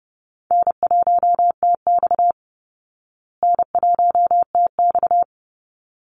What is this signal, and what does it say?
Morse code, heard as a keyed tone, about 30 wpm, 700 Hz. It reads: N1TX N1TX
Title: Easy Call Signs Repeated 2x - 30wpm